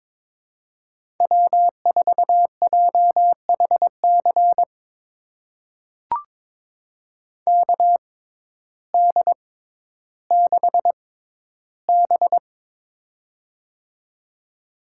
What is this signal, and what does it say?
Morse code, heard as a keyed tone, about 22 wpm, spaced as for 12 wpm, 700 Hz. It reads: W4JHC KD6B
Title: USA Call Signs - 6x Character Spacing - 22wpm